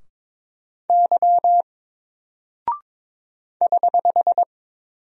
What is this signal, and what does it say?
Morse code, heard as a keyed tone, about 22 wpm, 700 Hz. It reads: Y <HH>
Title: Single Character (Letter:Number:Punctuation:Prosign) 22wpm